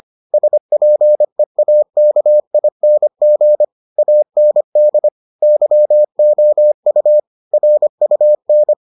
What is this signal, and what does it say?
Morse code, heard as a keyed tone, about 25 wpm, 600 Hz. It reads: SPEAKING AND YOU RUN